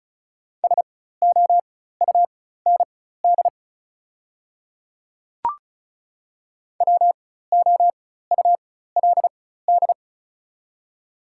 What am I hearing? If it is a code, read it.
Morse code, heard as a keyed tone, about 36 wpm, spaced as for 16 wpm, 700 Hz. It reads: SOUND WOULD
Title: 4x Character Spacing - Top 100 Words - 35wpm